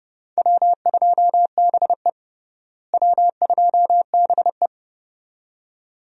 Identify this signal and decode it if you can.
Morse code, heard as a keyed tone, about 30 wpm, 700 Hz. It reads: W2BE W2BE
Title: USA Call Signs - 4 Characters - Repeated 2x - 30wpm